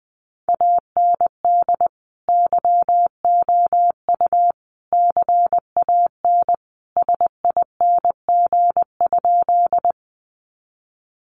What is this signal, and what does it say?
Morse code, heard as a keyed tone, about 20 wpm, 700 Hz. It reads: AND YOU CAN SING?
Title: Sentences from Top 400 Words 20wpm